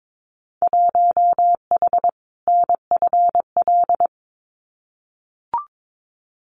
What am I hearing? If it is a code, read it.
Morse code, heard as a keyed tone, about 22 wpm, 700 Hz. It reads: 1H NFL
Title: Winter Field Day Exchanges 22wpm